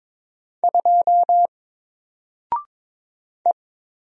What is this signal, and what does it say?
Morse code, heard as a keyed tone, about 22 wpm, 700 Hz. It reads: 2 E
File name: Single-Letter-Number-22wpm